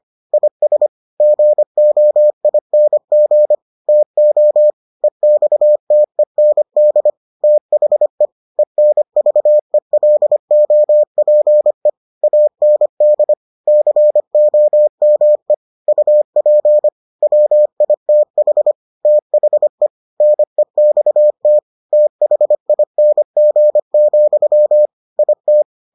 Morse code, heard as a keyed tone, about 25 wpm, 600 Hz. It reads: IS GOING TO EXTEND THE ENVELOPE AND COME UP WITH THE NEXT THING, IT